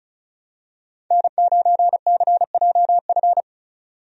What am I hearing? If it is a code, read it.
Morse code, heard as a keyed tone, about 36 wpm, 700 Hz. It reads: N9CJF
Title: USA Call Signs - 5 Characters - 35wpm